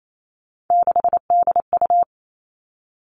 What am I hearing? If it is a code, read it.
Morse code, heard as a keyed tone, about 28 wpm, 700 Hz. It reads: 6DU
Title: All Cut Numbers - 3-Digit 28wpm